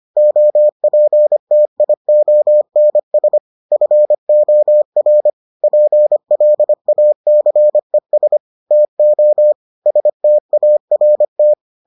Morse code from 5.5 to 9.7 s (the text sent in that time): PLACES TO